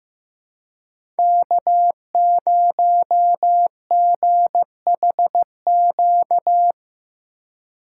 Morse code, heard as a keyed tone, about 15 wpm, 700 Hz. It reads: K0GHQ
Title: USA Call Signs - 5 Characters - 2x Character Spacing - 15wpm